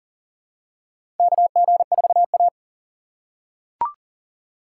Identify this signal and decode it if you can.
Morse code, heard as a keyed tone, about 40 wpm, 700 Hz. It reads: KG4A
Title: USA Call Signs - 6x Character Spacing - 40wpm